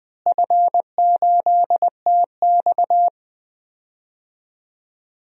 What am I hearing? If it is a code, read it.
Morse code, heard as a keyed tone, about 20 wpm, 700 Hz. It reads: F8TX
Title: DXCC-100 Call Signs - 4 Characters - Repeated 2x - 20wpm